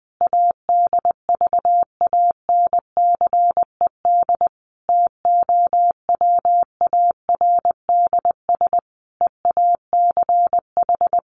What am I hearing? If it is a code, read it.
Morse code, heard as a keyed tone, about 20 wpm, 700 Hz. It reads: ADVANCED TOWARDS EACH
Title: Sentences from Top 1200 Words 20wpm